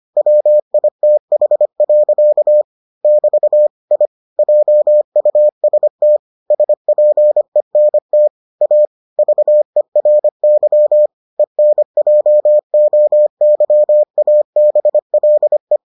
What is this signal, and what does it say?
Morse code, heard as a keyed tone, about 25 wpm, 600 Hz. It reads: WITH. <BT> I JUST SPENT A VERY ENJOYABLE